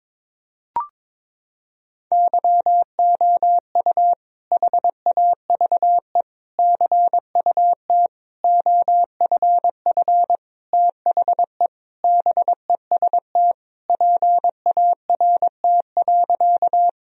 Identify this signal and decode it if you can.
Morse code, heard as a keyed tone, about 22 wpm, 700 Hz. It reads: YOU HAVE CUT OFF THE BEST PART.
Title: Sentences from Top 1000 Words - Review Part 1 - 22wpm